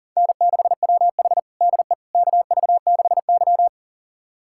Morse code, heard as a keyed tone, about 43 wpm, 690 Hz. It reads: N6WH DE KV6Y
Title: USA Callsign DE Callsign - 4 Characters - 40wpm